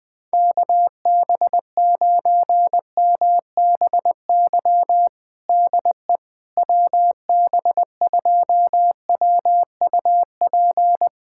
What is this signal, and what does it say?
Morse code, heard as a keyed tone, about 20 wpm, 700 Hz. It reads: KB9MBY DE WB2WUP
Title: USA Callsign DE Callsign - 6 Characters - 20wpm